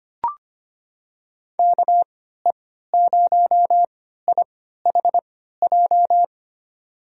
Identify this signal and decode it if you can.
Morse code, heard as a keyed tone, about 25 wpm, spaced as for 14 wpm, 700 Hz. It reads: KE0IHJ
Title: USA Call Signs - 6 Characters - 3x Character Spacing - 25wpm